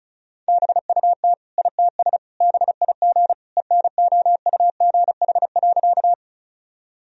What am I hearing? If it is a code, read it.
Morse code, heard as a keyed tone, about 36 wpm, 700 Hz. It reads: BUT ITS BIG ENOUGH.
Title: Sentences from Top 1000 Words - Review Part 1 - 35wpm